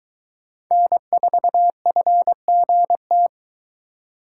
Morse code, heard as a keyed tone, about 20 wpm, 700 Hz. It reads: N4FGT